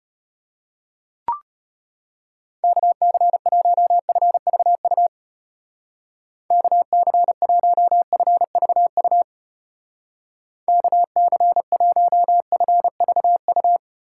Morse code, heard as a keyed tone, about 31 wpm, 700 Hz. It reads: KC1FVU KC1FVU KC1FVU
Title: Speed Racing - USA 2x3 Call Signs 25wpm